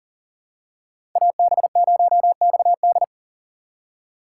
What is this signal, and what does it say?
Morse code, heard as a keyed tone, about 40 wpm, 700 Hz. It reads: AB0XD
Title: USA Call Signs - 5 Characters - 4x Character Spacing - 40wpm